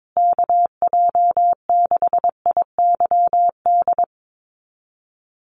Morse code, heard as a keyed tone, about 22 wpm, 700 Hz. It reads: KJ6IYD
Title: USA Call Signs Repeated 2x 22wpm